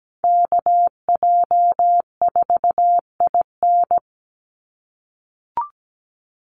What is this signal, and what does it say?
Morse code, heard as a keyed tone, about 17 wpm, 700 Hz. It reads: KJ4IN
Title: USA Call Signs 17wpm